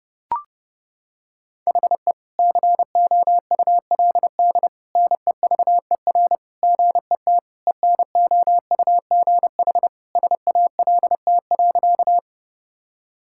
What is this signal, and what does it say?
Morse code, heard as a keyed tone, about 30 wpm, 700 Hz. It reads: HE COULD NEVER GET ENOUGH SALT.
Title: Sentences from Top 1000 Words - Review Part 5 - 30wpm